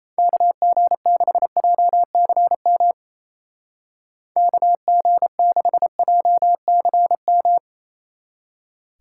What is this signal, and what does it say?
Morse code, heard as a keyed tone, about 33 wpm, 700 Hz. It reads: KG6JCM KG6JCM
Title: Speed Racing - USA 2x3 Call Signs 28wpm